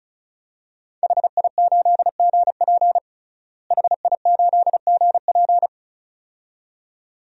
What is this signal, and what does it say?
Morse code, heard as a keyed tone, about 36 wpm, 700 Hz. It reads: HI8GP HI8GP
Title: DXCC-100 Call Signs - 5 Characters - Repeated 2x - 35wpm